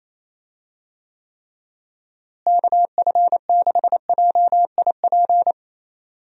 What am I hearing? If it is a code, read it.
Morse code, heard as a keyed tone, about 28 wpm, 700 Hz. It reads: KF6JIP
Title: USA Call Signs - 6 Characters - Repeated 3x - 28wpm